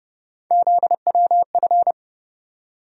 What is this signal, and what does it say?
Morse code, heard as a keyed tone, about 30 wpm, 700 Hz. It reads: ZWF
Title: USA Call Sign Postfixes - 3 Letters 30wpm